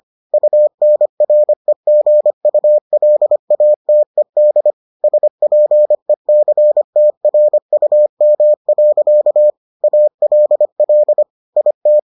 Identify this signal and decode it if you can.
Morse code, heard as a keyed tone, about 25 wpm, 600 Hz. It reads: UNREGULATED SPECTRUM. ALL IT